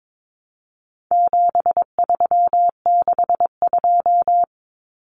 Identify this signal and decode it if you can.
Morse code, heard as a keyed tone, about 22 wpm, 700 Hz. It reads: 7362
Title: Numbers - 4-Digit 22wpm